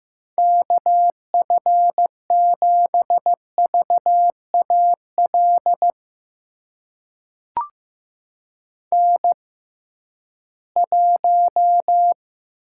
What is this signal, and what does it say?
Morse code, heard as a keyed tone, about 15 wpm, spaced as for 11 wpm, 700 Hz. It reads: KF7VAL N1
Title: USA Call Signs - 6x Character Spacing - 15wpm